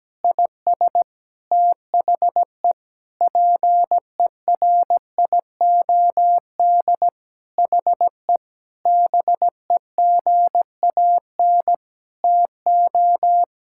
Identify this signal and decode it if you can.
Morse code, heard as a keyed tone, about 17 wpm, 700 Hz. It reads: IS THE PERIOD HE BEGAN TO
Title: Sentences from Top 1000 Words - Review Part 5 - 17wpm